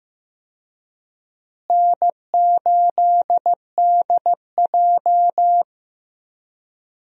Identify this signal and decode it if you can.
Morse code, heard as a keyed tone, about 15 wpm, 700 Hz. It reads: N8DJ
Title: USA Call Signs - 4 Characters - Repeated 2x - 15wpm